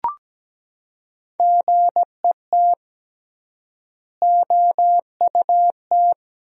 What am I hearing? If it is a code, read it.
Morse code, heard as a keyed tone, about 17 wpm, 700 Hz. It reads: GET OUT
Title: 3x Word-Spacing - Sentences from Top 200 Words - 17wpm